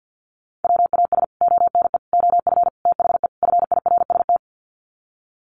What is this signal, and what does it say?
Morse code, heard as a keyed tone, about 50 wpm, 700 Hz. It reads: WAS ONE OF THE FIRST
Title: Sets of 5 Words 50wpm